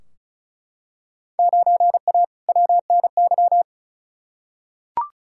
Morse code, heard as a keyed tone, about 34 wpm, 700 Hz. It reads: 9A WNY
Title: Field Day Exchanges Encore - 35wpm